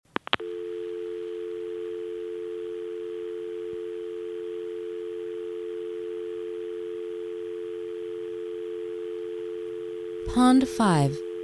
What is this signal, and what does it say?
Dial tone (North America), 350 + 440 Hz, with speech.
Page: dial-tone-tone-03